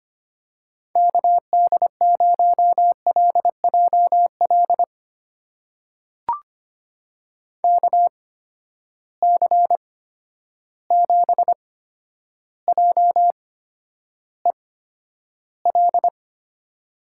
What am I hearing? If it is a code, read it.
Morse code, heard as a keyed tone, about 25 wpm, spaced as for 11 wpm, 700 Hz. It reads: KD0LJL KC7JEL
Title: USA Call Signs - 6 Characters - 8x Character Spacing - 25wpm